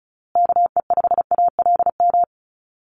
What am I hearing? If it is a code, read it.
Morse code, heard as a keyed tone, about 34 wpm, 700 Hz. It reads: KE5ALM
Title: USA Call Signs Repeated 3x 35wpm